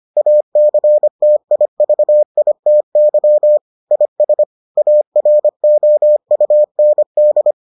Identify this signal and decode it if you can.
Morse code, heard as a keyed tone, about 25 wpm, 600 Hz. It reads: ACTIVITY IS AROUND